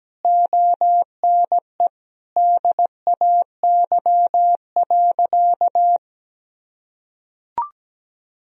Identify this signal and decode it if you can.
Morse code, heard as a keyed tone, about 17 wpm, 700 Hz. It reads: ONE DAY.
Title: Sentences from Top 1000 Words - Review Part 4 - 17wpm